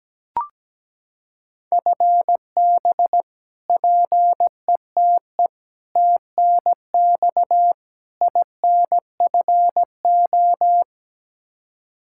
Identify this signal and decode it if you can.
Morse code, heard as a keyed tone, about 17 wpm, 700 Hz. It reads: FB PETE TNX INFO
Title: Practice QSO - Exchange 3 - More Info - 17wpm